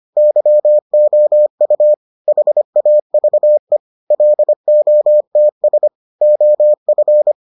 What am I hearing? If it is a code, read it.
Morse code, heard as a keyed tone, about 25 wpm, 600 Hz. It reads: YOU HAVE LOTS OF